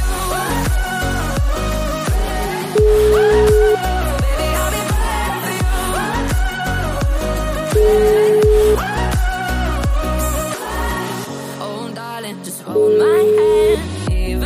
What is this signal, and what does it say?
Ringback tone (Europe, 425 Hz), with noise.